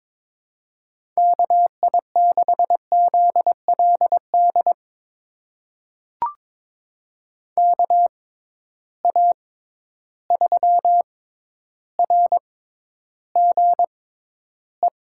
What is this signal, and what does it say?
Morse code, heard as a keyed tone, about 22 wpm, spaced as for 11 wpm, 700 Hz. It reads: KI6ZLD KA3RGE
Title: USA Call Signs - 6 Characters - 6x Character Spacing - 22wpm